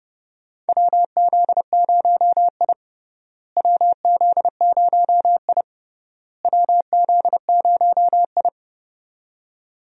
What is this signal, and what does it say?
Morse code, heard as a keyed tone, about 30 wpm, 700 Hz. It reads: WZ0I WZ0I WZ0I
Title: USA Call Signs - 4 Characters - Repeated 3x - 30wpm